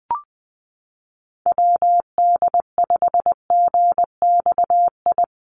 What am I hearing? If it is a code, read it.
Morse code, heard as a keyed tone, about 20 wpm, 700 Hz. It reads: WD5GXI
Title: USA Call Signs Repeated 2x 20wpm